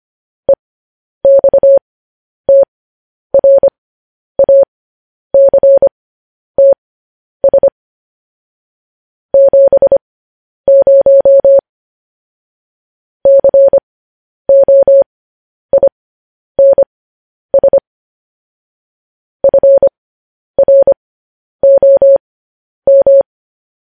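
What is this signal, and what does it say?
Morse code, heard as a keyed tone, about 25 wpm, spaced as for 10 wpm, 550 Hz. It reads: EXTRACTS 70 COINS FROM
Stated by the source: Morse code news headlines sent at 10 wpm.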